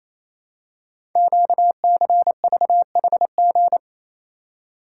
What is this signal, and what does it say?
Morse code, heard as a keyed tone, about 27 wpm, 700 Hz. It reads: QCVHG
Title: Code Groups - 5 Letters - 28wpm